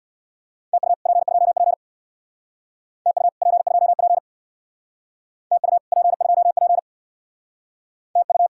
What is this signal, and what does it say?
Morse code, heard as a keyed tone, about 65 wpm, 700 Hz. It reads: TU P2P TU P2P TU P2P TU